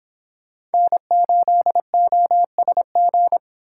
Morse code, heard as a keyed tone, about 30 wpm, 700 Hz. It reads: N8OSG